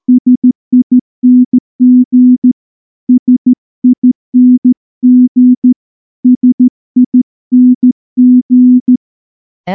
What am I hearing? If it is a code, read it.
Morse code, heard as a keyed tone, about 14 wpm, 250 Hz. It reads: SING SING SING
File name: morse-training-15WPM